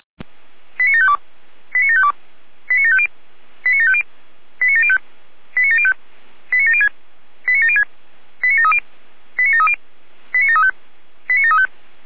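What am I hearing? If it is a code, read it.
Five-tone selective call (ZVEI), heard as quick runs of five tones, tones of about 80 ms. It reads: 89752 89752 89750 89750 89705 89705 89706 89706 89730 89730 89735 89735
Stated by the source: ZVEI five-tone selective call received by SDR, channel 455.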